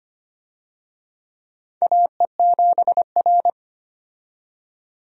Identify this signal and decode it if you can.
Morse code, heard as a keyed tone, about 25 wpm, 700 Hz. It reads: AE7R
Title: USA Call Signs - 4 Characters - 25wpm